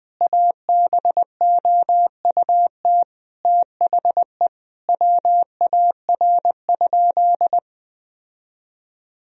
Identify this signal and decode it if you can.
Morse code, heard as a keyed tone, about 21 wpm, 700 Hz. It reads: ABOUT THE WAR?
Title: Sentences from Top 1000 Words - Review Part 4 - 20wpm